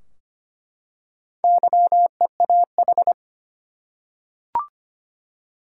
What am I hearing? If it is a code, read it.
Morse code, heard as a keyed tone, about 25 wpm, 700 Hz. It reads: YEAH
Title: Four Letter Words - 25wpm